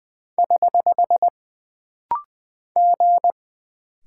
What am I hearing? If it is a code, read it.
Morse code, heard as a keyed tone, about 20 wpm, 700 Hz. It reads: <HH> G
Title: Single Character (Letter:Number:Punctuation:Prosign) - Rapid-Fire - 20wpm